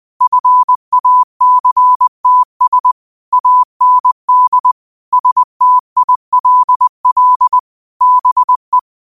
Morse code, heard as a keyed tone, about 20 wpm, 1000 Hz. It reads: FACTS AND STILL BE
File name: qotdfiles / 20wpm